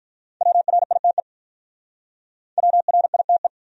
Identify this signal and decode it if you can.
Morse code, heard as a keyed tone, about 46 wpm, 700 Hz. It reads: WRITE WRITE